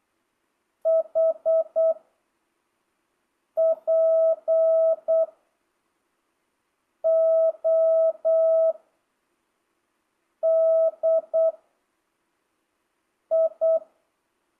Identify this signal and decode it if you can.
Morse code, heard as a keyed tone, about 8 wpm, 660 Hz. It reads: H P O D I
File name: jh8cbh / 2025cw